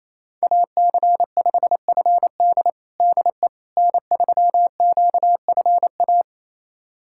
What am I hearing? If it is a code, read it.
Morse code, heard as a keyed tone, about 28 wpm, 700 Hz. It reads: AC5FD DE N3QFA
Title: USA Callsign DE Callsign - 5 Characters - 28wpm